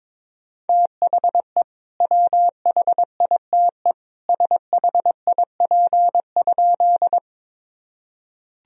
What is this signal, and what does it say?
Morse code, heard as a keyed tone, about 21 wpm, 700 Hz. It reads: THE WHITE SHIP?
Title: Sentences from Top 1000 Words - Review Part 2 - 22wpm